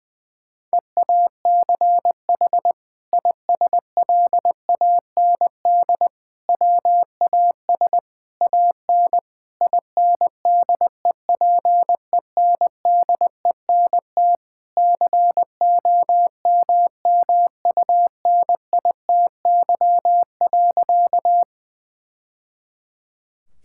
Morse code, heard as a keyed tone, about 20 wpm, 700 Hz. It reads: EACH ISLAND WAS AN INDEPENDENT COMMUNITY.
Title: Sentences from Top 1000 Words - Review Part 6 - 20wpm